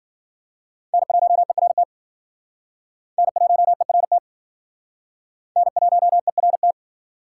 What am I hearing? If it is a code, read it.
Morse code, heard as a keyed tone, about 53 wpm, 700 Hz. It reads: N1ERT N1ERT N1ERT